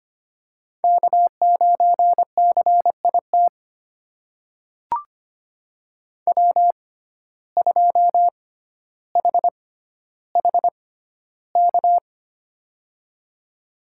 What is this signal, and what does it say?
Morse code, heard as a keyed tone, about 25 wpm, spaced as for 13 wpm, 700 Hz. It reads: K9CIT W2HHK
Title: USA Call Signs - 6x Character Spacing - 25wpm